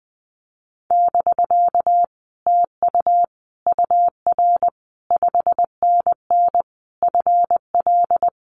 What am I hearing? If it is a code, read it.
Morse code, heard as a keyed tone, about 20 wpm, 700 Hz. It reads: <BK> TU UR 5NN FL
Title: Common QSO Phrases Encore 20wpm